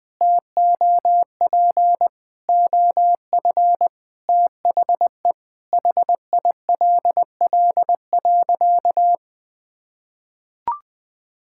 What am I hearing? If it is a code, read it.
Morse code, heard as a keyed tone, about 21 wpm, 700 Hz. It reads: TOP OF THE HILL.
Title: Sentences with 4-Word Phrases - 20wpm